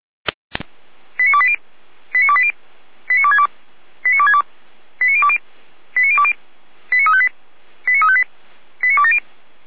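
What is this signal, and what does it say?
Five-tone selective call (ZVEI), heard as quick runs of five tones, tones of about 70 ms. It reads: 89280 89280 89262 89262 89020 89020 89358 89358 89270
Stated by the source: ZVEI five-tone call received by SDR.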